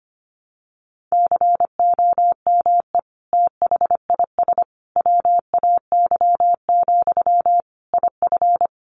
Morse code, heard as a keyed tone, about 25 wpm, 700 Hz. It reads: COME THIS WAY, IF
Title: Sentences from Top 1500 Words 25wpm